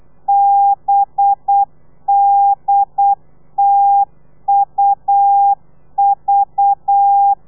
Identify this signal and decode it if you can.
Morse code, heard as a keyed tone, about 8 wpm, 800 Hz. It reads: BDTUV
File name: telegrafia / i2tfj / Lez4 35